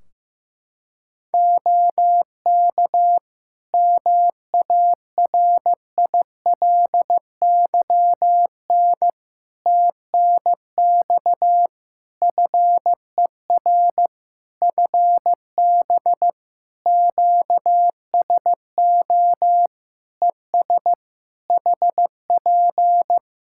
Morse code, heard as a keyed tone, about 15 wpm, 700 Hz. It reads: OK MARILYN TNX FER FB QSO ES HP
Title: Practice QSO - Exchange 4 - Ending - 15wpm